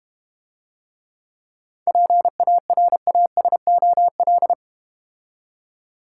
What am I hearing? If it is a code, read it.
Morse code, heard as a keyed tone, about 30 wpm, 700 Hz. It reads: PARASOL
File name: Easy Words - Speed-Racing with Hints - 25wpm